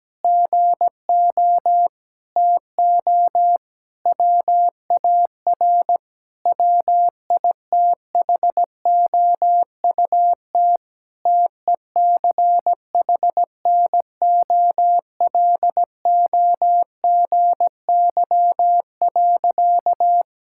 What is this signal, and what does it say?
Morse code, heard as a keyed tone, about 17 wpm, 700 Hz. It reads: GO TO WAR WITHOUT TECHNOLOGY.
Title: Sentences from Top 1000 Words - Review Part 5 - 17wpm